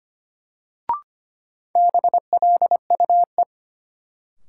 Morse code, heard as a keyed tone, about 25 wpm, 700 Hz. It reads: BLUE
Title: Sentences from Top 1200 Words - New Word Drill Alt - 25wpm